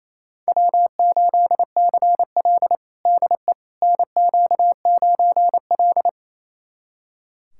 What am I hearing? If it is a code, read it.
Morse code, heard as a keyed tone, about 27 wpm, 700 Hz. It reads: W8CL DE NQ9L
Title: USA Callsign DE Callsign - 4 Characters - 28wpm